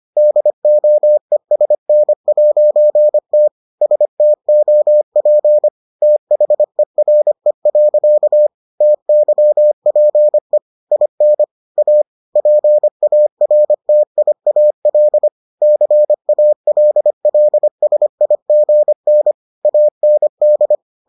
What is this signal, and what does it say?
Morse code, heard as a keyed tone, about 25 wpm, 600 Hz. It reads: DOESN'T STOP THERE. TYPE IN A PARTIAL CALLSIGN AND